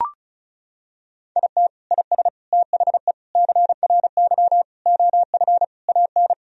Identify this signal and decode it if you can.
Morse code, heard as a keyed tone, about 34 wpm, 700 Hz. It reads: IT IS THE CRY OF AN
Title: Sentences from Top 1000 Words - Review Part 2 - 35wpm